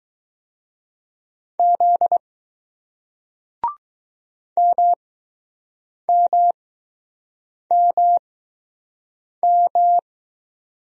Morse code, heard as a keyed tone, about 20 wpm, spaced as for 6 wpm, 700 Hz. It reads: Z MMMM